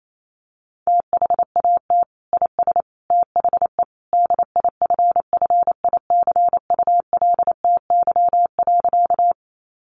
Morse code, heard as a keyed tone, about 28 wpm, 700 Hz. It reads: THAT IS THE DIFFICULTY.
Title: Sentences from Top 1500 Words 28wpm